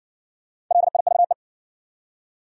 Morse code, heard as a keyed tone, about 59 wpm, 700 Hz. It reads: LIVE